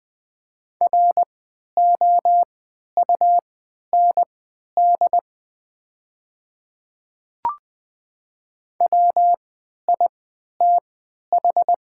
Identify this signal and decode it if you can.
Morse code, heard as a keyed tone, about 20 wpm, spaced as for 11 wpm, 690 Hz. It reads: ROUND WITH
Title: 3x Character Spacing - Top 200 Words - 20wpm